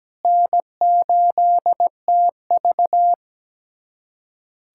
Morse code, heard as a keyed tone, about 17 wpm, 700 Hz. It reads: N8TV
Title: USA Callsign DE Callsign - 4 Characters - 17wpm